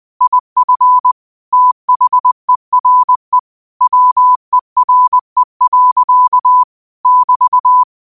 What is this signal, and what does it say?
Morse code, heard as a keyed tone, about 20 wpm, 1000 Hz. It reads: IF THERE WERE. <BT>
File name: qotdfiles / 20wpm